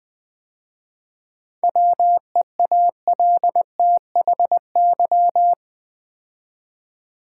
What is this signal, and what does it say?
Morse code, heard as a keyed tone, about 20 wpm, 700 Hz. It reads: WEALTHY
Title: 7 Letter Words - First 3 Letters Given - 20wpm